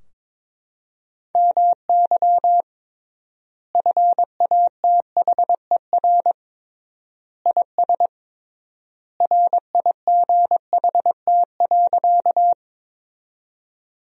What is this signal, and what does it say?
Morse code, heard as a keyed tone, about 21 wpm, 700 Hz. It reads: MY FATHER IS RIGHT.
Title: 3x Word-Spacing - Sentences from Top 300 Words - 22wpm